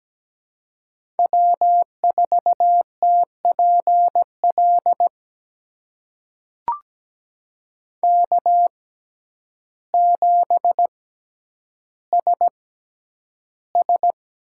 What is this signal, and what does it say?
Morse code, heard as a keyed tone, about 17 wpm, spaced as for 9 wpm, 700 Hz. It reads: W4TPL K7SS
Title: USA Call Signs - 5 Characters - 6x Character Spacing - 17wpm